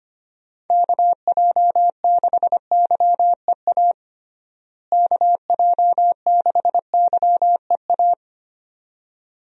Morse code, heard as a keyed tone, about 25 wpm, 700 Hz. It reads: KJ6YEA KJ6YEA
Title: USA Call Signs - 6 Characters - Repeated 2x - 25wpm